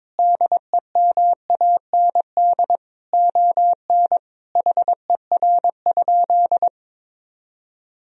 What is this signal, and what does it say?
Morse code, heard as a keyed tone, about 22 wpm, 700 Hz. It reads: DEMAND ON HER?
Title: Sentences from Top 1000 Words - Review Part 4 - 22wpm